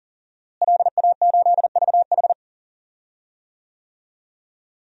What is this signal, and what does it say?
Morse code, heard as a keyed tone, about 40 wpm, 700 Hz. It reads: LA8VH
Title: DXCC-100 Call Signs - 5 Characters - Repeated 2x - 40wpm